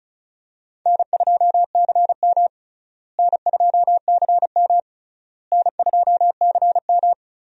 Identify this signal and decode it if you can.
Morse code, heard as a keyed tone, about 36 wpm, 700 Hz. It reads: N2CM N2CM N2CM
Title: USA Call Signs - 4 Characters - Repeated 3x - 35wpm